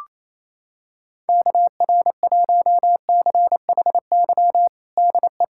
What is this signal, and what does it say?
Morse code, heard as a keyed tone, about 28 wpm, 700 Hz. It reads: KR1CHY DE
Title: USA Callsign DE Callsign - 6 Characters - 28wpm